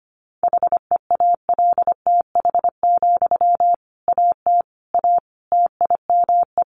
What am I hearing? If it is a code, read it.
Morse code, heard as a keyed tone, about 25 wpm, 700 Hz. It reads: HEALTH, AT A TIME
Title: Sentences from Top 600 Words 25wpm